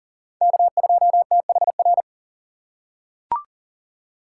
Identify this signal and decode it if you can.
Morse code, heard as a keyed tone, about 40 wpm, 700 Hz. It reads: K2THR
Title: USA Call Signs - 5 Characters - 8x Character Spacing - 40wpm